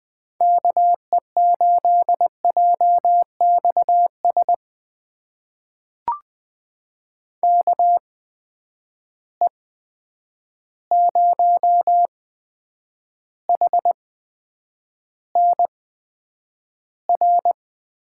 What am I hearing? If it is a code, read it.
Morse code, heard as a keyed tone, about 20 wpm, spaced as for 9 wpm, 700 Hz. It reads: KE8JXS KE0HNR
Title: USA Call Signs - 6 Characters - 8x Character Spacing - 20wpm